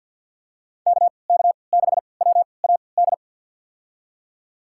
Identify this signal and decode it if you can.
Morse code, heard as a keyed tone, about 51 wpm, spaced as for 28 wpm, 700 Hz. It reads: KK6WAD
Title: USA Call Signs - 6 Characters - 3x Character Spacing - 50wpm